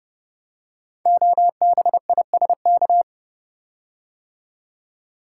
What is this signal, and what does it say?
Morse code, heard as a keyed tone, about 30 wpm, 700 Hz. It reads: OBISK
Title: Code Groups - 5 Letters - 30wpm